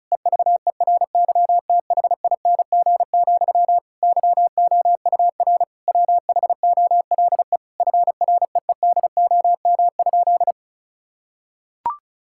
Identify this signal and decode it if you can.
Morse code, heard as a keyed tone, about 36 wpm, 700 Hz. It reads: EVERYTHING, YOUR WHOLE FREEDOM?
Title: Sentences from Top 1000 Words - Review Part 5 - 35wpm